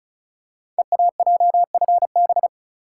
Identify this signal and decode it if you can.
Morse code, heard as a keyed tone, about 36 wpm, 700 Hz. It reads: EAJFB
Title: Code Groups - 5 Letter:Numbers - 35wpm